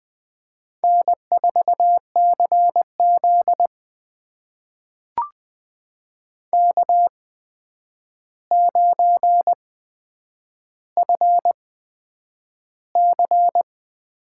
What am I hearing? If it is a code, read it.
Morse code, heard as a keyed tone, about 20 wpm, spaced as for 9 wpm, 700 Hz. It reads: N4CZ K9FC
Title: USA Call Signs - 4 Characters - 8x Character Spacing - 20wpm